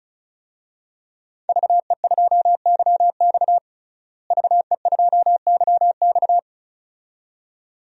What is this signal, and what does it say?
Morse code, heard as a keyed tone, about 36 wpm, 700 Hz. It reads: VE2YX VE2YX
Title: DXCC-100 Call Signs - 5 Characters - Repeated 2x - 35wpm